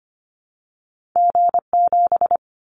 Morse code, heard as a keyed tone, about 25 wpm, 700 Hz. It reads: G7
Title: Callsign Prefixes for Top 25 DXCC - 25wpm